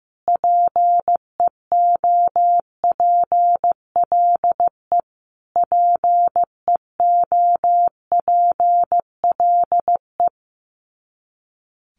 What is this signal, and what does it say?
Morse code, heard as a keyed tone, about 15 wpm, 700 Hz. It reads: PEOPLE PEOPLE
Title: Top 500 Words Repeated 15wpm